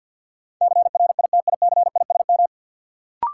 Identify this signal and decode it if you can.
Morse code, heard as a keyed tone, about 53 wpm, 700 Hz. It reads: CRITICISM
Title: Sentences from Top 1300 Words - New Word Drill Alt - 50wpm